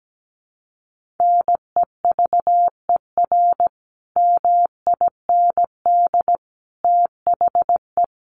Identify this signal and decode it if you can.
Morse code, heard as a keyed tone, about 17 wpm, 700 Hz. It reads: NEVER MIND THE
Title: Sentences from Top 1400 Words 17wpm